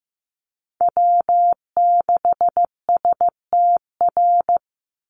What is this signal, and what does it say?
Morse code, heard as a keyed tone, about 15 wpm, 700 Hz. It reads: W6STR
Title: NA SOTA Chaser Callsigns 15wpm